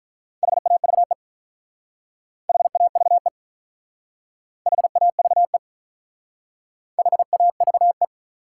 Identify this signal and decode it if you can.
Morse code, heard as a keyed tone, about 46 wpm, 700 Hz. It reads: HAVE HAVE HAVE HAVE